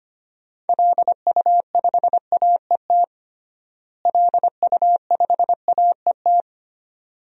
Morse code, heard as a keyed tone, about 25 wpm, 700 Hz. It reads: LU5AET LU5AET
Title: DXCC-100 Call Signs - 6 Characters - Repeated 2x - 25wpm